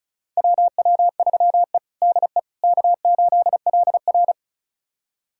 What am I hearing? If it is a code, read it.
Morse code, heard as a keyed tone, about 36 wpm, 700 Hz. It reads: WW3E DE K8LR
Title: USA Callsign DE Callsign - 4 Characters - 35wpm